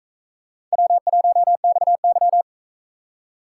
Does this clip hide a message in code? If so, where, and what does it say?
Yes: Morse code from 0.7 to 2.4 s, heard as a keyed tone, about 40 wpm, 700 Hz. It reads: W1XY